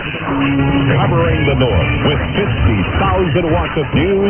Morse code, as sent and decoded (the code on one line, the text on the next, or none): none